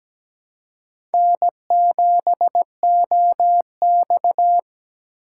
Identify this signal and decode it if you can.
Morse code, heard as a keyed tone, about 17 wpm, 700 Hz. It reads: N7OX
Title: USA Call Signs - 4 Characters - 2x Character Spacing - 17wpm